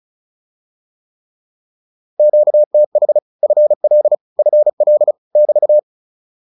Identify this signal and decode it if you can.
Morse code, heard as a keyed tone, about 34 wpm, 600 Hz. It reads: QTH FL FL <BT>
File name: QSO207 35